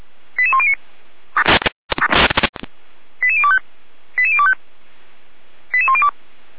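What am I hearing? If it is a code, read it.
Five-tone selective call (ZVEI), heard as quick runs of five tones, tones of about 70 ms. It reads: 88109 88025 88025 88102
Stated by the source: ZVEI five-tone selective call 88103, received by SDR, from feuerwehr-orlamuende.org (Feuerwehr Orlamünde).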